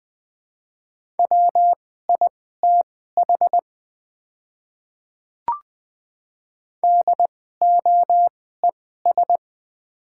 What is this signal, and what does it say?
Morse code, heard as a keyed tone, about 20 wpm, spaced as for 15 wpm, 700 Hz. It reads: WITH DOES
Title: 2x Character Spacing - Top 200 Words - 20wpm